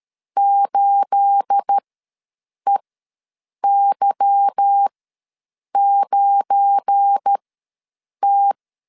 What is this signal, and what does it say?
Morse code, heard as a keyed tone, about 13 wpm, spaced as for 7 wpm, 800 Hz. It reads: 8EY9T
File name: LEZIONI / 9 28 ws 12wpm 3354